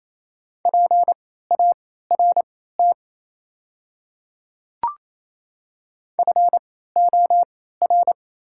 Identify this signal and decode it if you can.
Morse code, heard as a keyed tone, about 28 wpm, spaced as for 16 wpm, 690 Hz. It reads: PART FOR
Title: 3x Character Spacing - Top 500 Words - 28wpm